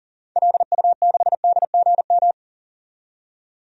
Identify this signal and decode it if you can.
Morse code, heard as a keyed tone, about 40 wpm, 700 Hz. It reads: LU6DGM
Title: DXCC-100 Call Signs - 6 Characters - 40wpm